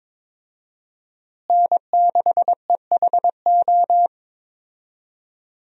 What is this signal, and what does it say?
Morse code, heard as a keyed tone, about 22 wpm, 700 Hz. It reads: N6EHO
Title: USA Call Signs - 5 Characters - Repeated 2x - 22wpm